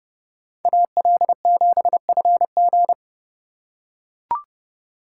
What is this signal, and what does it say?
Morse code, heard as a keyed tone, about 29 wpm, 700 Hz. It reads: AL7FG